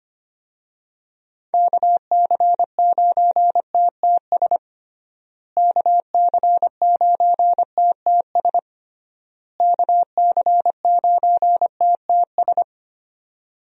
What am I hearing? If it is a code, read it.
Morse code, heard as a keyed tone, about 25 wpm, 700 Hz. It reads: KC9TTS KC9TTS KC9TTS
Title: USA Call Signs - 6 Characters - Repeated 3x - 25wpm